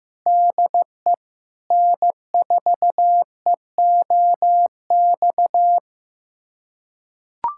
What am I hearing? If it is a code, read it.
Morse code, heard as a keyed tone, about 16 wpm, 700 Hz. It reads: DE N4EOX
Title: USA Callsign DE Callsign - 5 Characters - 15wpm